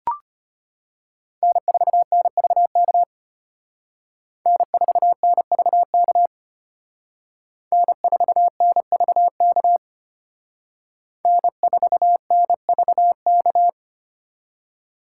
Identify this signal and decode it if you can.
Morse code, heard as a keyed tone, about 33 wpm, 700 Hz. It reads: N4NVK N4NVK N4NVK N4NVK